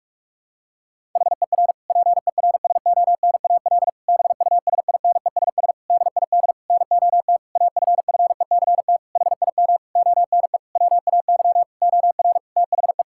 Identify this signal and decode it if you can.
Morse code, heard as a keyed tone, about 46 wpm, 700 Hz. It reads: HER PERSONAL BUSINESS DID NOT AFFECT HIM ONE WAY OR THE
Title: Sentences from Top 1000 Words - Review Part 6 - 45wpm